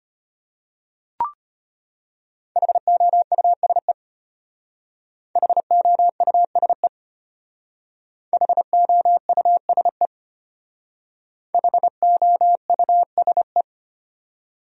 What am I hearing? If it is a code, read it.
Morse code, heard as a keyed tone, about 31 wpm, 700 Hz. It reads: HOUSE HOUSE HOUSE HOUSE
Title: Speed-Racing Top 500 Words 25wpm